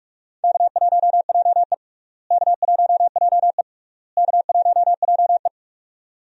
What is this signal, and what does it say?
Morse code, heard as a keyed tone, about 44 wpm, 700 Hz. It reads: K1JE K1JE K1JE
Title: USA Call Signs - 4 Characters - Repeated 3x - 45wpm